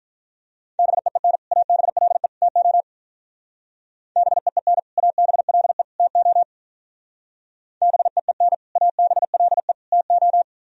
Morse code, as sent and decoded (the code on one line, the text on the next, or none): -... . -. / .- -... .-.. / - --- / -... . -. / .- -... .-.. / - --- / -... . -. / .- -... .-.. / - ---
BEEN ABLE TO BEEN ABLE TO BEEN ABLE TO